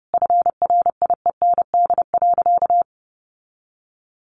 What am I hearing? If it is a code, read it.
Morse code, heard as a keyed tone, about 30 wpm, 700 Hz. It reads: FRIEND.
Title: Sentences from Top 300 Words 30wpm